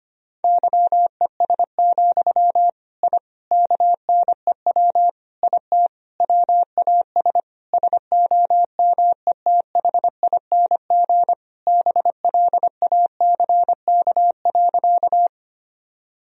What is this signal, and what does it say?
Morse code, heard as a keyed tone, about 25 wpm, 700 Hz. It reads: YES, I KNEW IT WAS SOMETHING BLACK.
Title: Sentences from Top 1000 Words - Review Part 2 - 25wpm